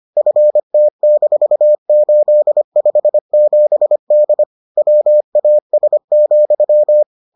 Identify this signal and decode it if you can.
Morse code, heard as a keyed tone, about 25 wpm, 600 Hz. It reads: FT-857D WAS,